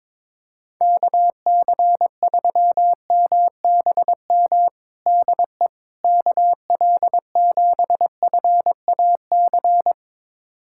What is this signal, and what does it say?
Morse code, heard as a keyed tone, about 21 wpm, 700 Hz. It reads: KC3MBM DE KL7FAC
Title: USA Callsign DE Callsign 22wpm